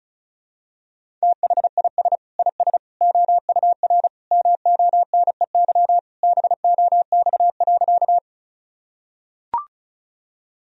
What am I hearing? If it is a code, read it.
Morse code, heard as a keyed tone, about 36 wpm, 700 Hz. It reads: THIS IS OUR MONEY BOX.
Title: Sentences from Top 1000 Words - Review Part 2 - 35wpm